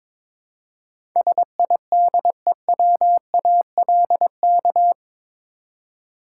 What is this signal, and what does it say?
Morse code, heard as a keyed tone, about 21 wpm, 700 Hz. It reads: SIDEWALK
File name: practice / 22wpm / Eight Letter Words - 22wpm